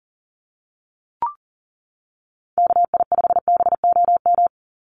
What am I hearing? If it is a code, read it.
Morse code, heard as a keyed tone, about 40 wpm, 700 Hz. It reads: KI5BOM
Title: USA Call Signs 40wpm